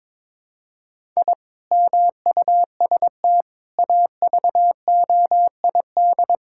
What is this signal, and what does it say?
Morse code, heard as a keyed tone, about 22 wpm, 700 Hz. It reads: I MUST AVOID
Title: Sentences from Top 1000 Words - Review Part 6 - 22wpm